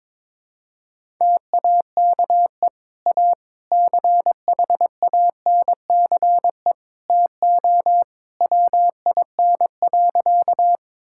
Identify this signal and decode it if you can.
Morse code, heard as a keyed tone, about 22 wpm, 700 Hz. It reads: TAKE A CHANCE TO WIN.
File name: practice / 22wpm / Sentences with 3-Word Phrases - 22wpm